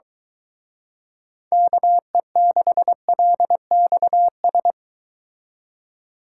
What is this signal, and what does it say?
Morse code, heard as a keyed tone, about 23 wpm, 700 Hz. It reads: KE6LXS